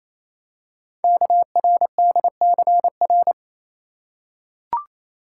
Morse code, heard as a keyed tone, about 27 wpm, 700 Hz. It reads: KRDCR
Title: Code Groups - 5 Letters - 28wpm